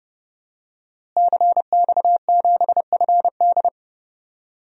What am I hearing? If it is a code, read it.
Morse code, heard as a keyed tone, about 30 wpm, 700 Hz. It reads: CX7FD
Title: DXCC-100 Call Signs - 5 Characters - Repeated 2x - 30wpm